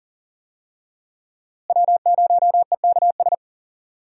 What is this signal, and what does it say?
Morse code, heard as a keyed tone, about 40 wpm, 700 Hz. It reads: W0EKS
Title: USA Call Signs - 5 Characters - Repeated 3x - 40wpm